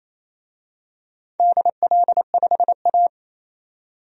Morse code, heard as a keyed tone, about 27 wpm, 700 Hz. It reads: DL5A
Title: DXCC-100 Call Signs - 4 Characters - 28wpm